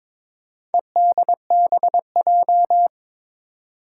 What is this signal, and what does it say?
Morse code, heard as a keyed tone, about 22 wpm, 700 Hz. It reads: EDBJ
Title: Four Challenging Letter-Number Combinations 22wpm